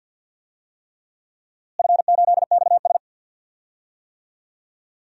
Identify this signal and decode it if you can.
Morse code, heard as a keyed tone, about 52 wpm, 700 Hz. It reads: F8XS
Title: DXCC-100 Call Signs - 4 Characters - Repeated 3x - 50wpm